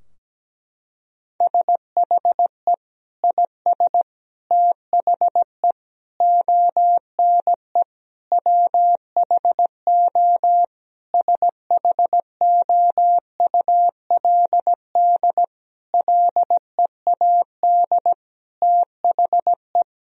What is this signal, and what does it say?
Morse code, heard as a keyed tone, about 17 wpm, 700 Hz. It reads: SHE IS THE ONE WHO SHOULD LEAD THE
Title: Sentences from Top 1000 Words - Review Part 2 - 17wpm